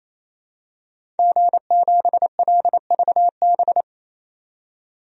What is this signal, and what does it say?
Morse code, heard as a keyed tone, about 28 wpm, 700 Hz. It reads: G7LVB
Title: DXCC-100 Call Signs - 5 Characters - 28wpm